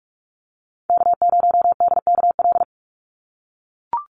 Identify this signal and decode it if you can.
Morse code, heard as a keyed tone, about 44 wpm, 690 Hz. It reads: K0DKL
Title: USA Call Signs 45wpm